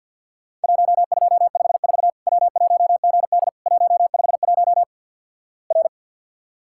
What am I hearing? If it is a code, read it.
Morse code, heard as a keyed tone, about 50 wpm, 700 Hz. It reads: 1254 W1GD 151 R
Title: Worked All Europe DX Contest - QTC Exchange with Repeats - 50wpm